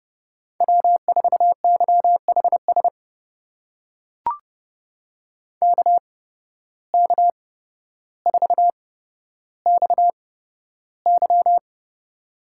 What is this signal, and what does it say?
Morse code, heard as a keyed tone, about 32 wpm, spaced as for 13 wpm, 700 Hz. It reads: W4YHS KK4XY
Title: USA Call Signs - 8x Character Spacing - 30wpm